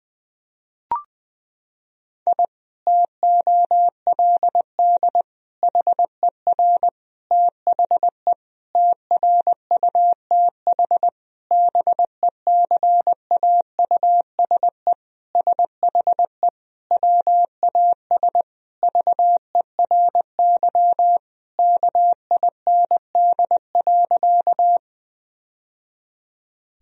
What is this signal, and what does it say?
Morse code, heard as a keyed tone, about 21 wpm, 700 Hz. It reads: I TOLD HER THE TRUTH BECAUSE SHE WAS VERY KIND.
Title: Sentences from Top 1000 Words - Review Part 3 - 20wpm